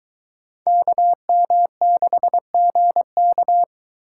Morse code, heard as a keyed tone, about 22 wpm, 700 Hz. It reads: KM6GK